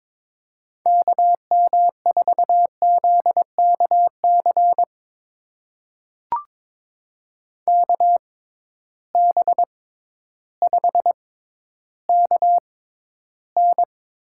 Morse code, heard as a keyed tone, about 22 wpm, spaced as for 12 wpm, 700 Hz. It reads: KM4ZKC KB5KN
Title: USA Call Signs - 6 Characters - 6x Character Spacing - 22wpm